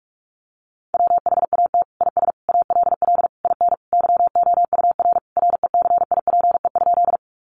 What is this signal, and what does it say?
Morse code, heard as a keyed tone, about 44 wpm, 700 Hz. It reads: WHAT IS ALL IN YOUR RECIPE?
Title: Sentences from Top 1200 Words 45wpm